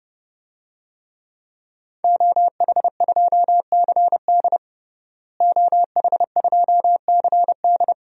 Morse code, heard as a keyed tone, about 30 wpm, 700 Hz. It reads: OH2CD OH2CD
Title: DXCC-100 Call Signs - 5 Characters - Repeated 2x - 30wpm